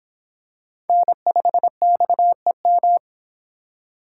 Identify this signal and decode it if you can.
Morse code, heard as a keyed tone, about 26 wpm, 700 Hz. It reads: N5XEM